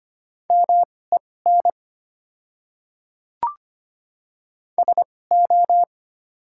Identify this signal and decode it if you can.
Morse code, heard as a keyed tone, about 25 wpm, spaced as for 18 wpm, 700 Hz. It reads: MEN SO
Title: 2x Character Spacing - Top 200 Words - 25wpm